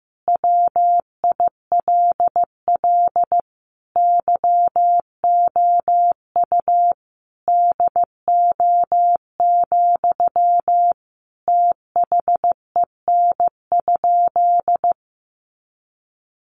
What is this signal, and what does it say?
Morse code, heard as a keyed tone, about 15 wpm, 700 Hz. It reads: WILL YOU DO, THEN?
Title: Sentences from Top 100 Words 15wpm